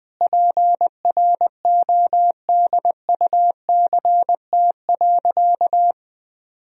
Morse code, heard as a keyed tone, about 21 wpm, 700 Hz. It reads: PRODUCT.
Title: Sentences from Top 1000 Words - Review Part 2 - 20wpm